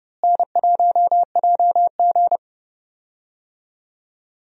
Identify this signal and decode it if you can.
Morse code, heard as a keyed tone, about 30 wpm, 700 Hz. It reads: N1JG
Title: USA Call Signs - 4 Characters - Repeated 3x - 30wpm